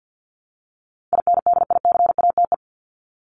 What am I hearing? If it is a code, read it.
Morse code, heard as a keyed tone, about 50 wpm, 700 Hz. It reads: INDICATE
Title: ED Suffix - Top 100 Words with Base Words 50wpm